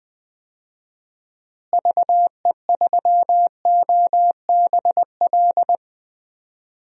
Morse code, heard as a keyed tone, about 20 wpm, 700 Hz. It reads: VE3OBL
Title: DXCC-100 Call Signs - 6 Characters - Repeated 2x - 20wpm